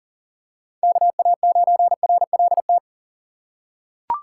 Morse code, heard as a keyed tone, about 40 wpm, 700 Hz. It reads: KA9RLT
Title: USA Call Signs - 6 Characters - 6x Character Spacing - 40wpm